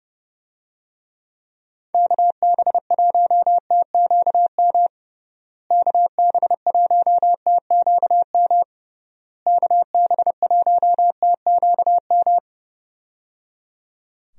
Morse code, heard as a keyed tone, about 30 wpm, 700 Hz. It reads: KB1TQM KB1TQM KB1TQM
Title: USA Call Signs - 6 Characters - Repeated 3x - 30wpm